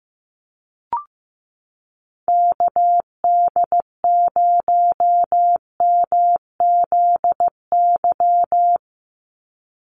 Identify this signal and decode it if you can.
Morse code, heard as a keyed tone, about 15 wpm, 700 Hz. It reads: KD0MZY
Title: USA Call Signs 15wpm